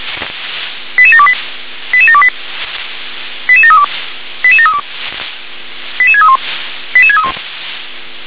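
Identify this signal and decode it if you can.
Five-tone selective call (ZVEI), heard as quick runs of five tones, tones of about 70 ms. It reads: 88728 88728 88732 88732 88631 88631
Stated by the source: ZVEI five-tone selective call 88510 received by SDR.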